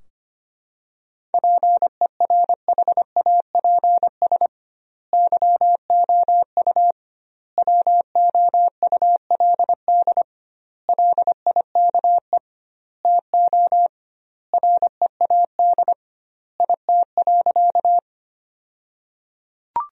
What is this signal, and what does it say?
Morse code, heard as a keyed tone, about 25 wpm, 700 Hz. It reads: PERHAPS YOU WOULD LIKE TO READ IT.
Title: 2x Word-Spacing - Sentences from Top 500 Words - 25wpm